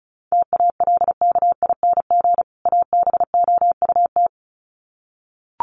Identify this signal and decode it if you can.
Morse code, heard as a keyed tone, about 34 wpm, 700 Hz. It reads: TALKING ABOUT
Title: Sets of 2 Words 35wpm